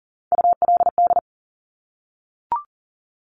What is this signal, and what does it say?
Morse code, heard as a keyed tone, about 38 wpm, 700 Hz. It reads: ULD
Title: Top Three Letter Combinations in Words 40wpm